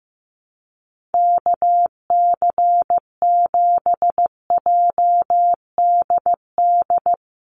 Morse code, heard as a keyed tone, about 15 wpm, 700 Hz. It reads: KC7JDD
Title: USA Call Signs 15wpm